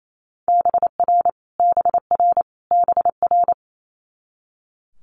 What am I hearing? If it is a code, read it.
Morse code, heard as a keyed tone, about 28 wpm, 700 Hz. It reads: BR BR BR
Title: Multi-letter Phonemes Repeated 3x 28wpm